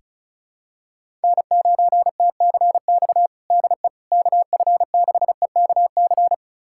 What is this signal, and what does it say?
Morse code, heard as a keyed tone, about 36 wpm, 700 Hz. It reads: N9TCX DE KF6EKC
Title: USA Callsign DE Callsign 35wpm